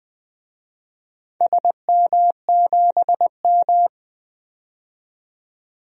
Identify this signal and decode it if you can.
Morse code, heard as a keyed tone, about 21 wpm, 700 Hz. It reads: SM7M
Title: DXCC-100 Call Signs - 4 Characters - 20wpm